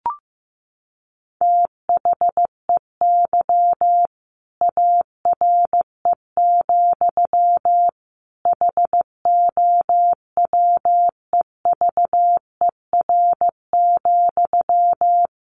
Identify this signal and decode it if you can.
Morse code, heard as a keyed tone, about 15 wpm, 700 Hz. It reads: THEY ARE, HOWEVER,
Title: Sentences from Top 1200 Words 15wpm